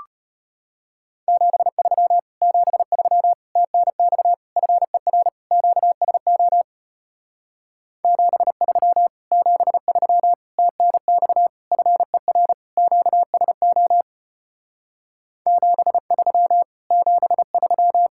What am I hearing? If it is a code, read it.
Morse code, heard as a keyed tone, about 33 wpm, 700 Hz. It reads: 73 73 TNX FER QSO 73 73 TNX FER QSO 73 73